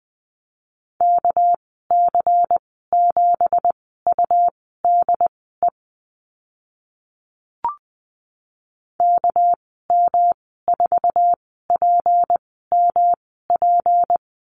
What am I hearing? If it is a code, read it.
Morse code, heard as a keyed tone, about 20 wpm, spaced as for 15 wpm, 700 Hz. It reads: KC7UDE KM4PMP
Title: 213 Lesson 34g - Number 8 - Call Signs 20wpm